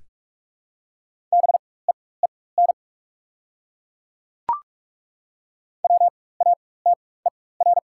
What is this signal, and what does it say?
Morse code, heard as a keyed tone, about 42 wpm, spaced as for 21 wpm, 690 Hz. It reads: BEEN WATER
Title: 4x Character Spacing - Top 500 Words - 45wpm